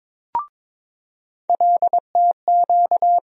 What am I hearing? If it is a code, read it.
Morse code, heard as a keyed tone, about 22 wpm, 700 Hz. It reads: LTQ
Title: USA Call Sign Postfixes - 3 Letters 22wpm